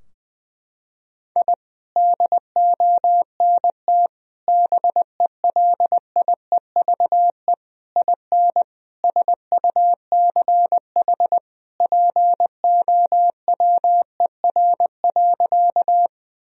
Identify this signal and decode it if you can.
Morse code, heard as a keyed tone, about 20 wpm, 700 Hz. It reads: I DONT BELIEVE IN SUCH POWER.
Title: Sentences from Top 1000 Words - Review Part 2 - 20wpm